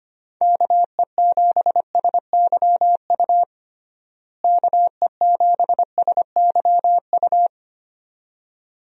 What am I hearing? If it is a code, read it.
Morse code, heard as a keyed tone, about 25 wpm, 700 Hz. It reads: KE7SYU KE7SYU
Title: USA Call Signs - 6 Characters - Repeated 2x - 25wpm